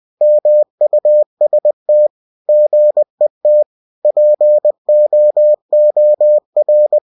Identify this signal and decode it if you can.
Morse code, heard as a keyed tone, about 20 wpm, 600 Hz. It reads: MUST GET POOR
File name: cw-books / winnie-the-pooh / winnie-20wpm-page-0048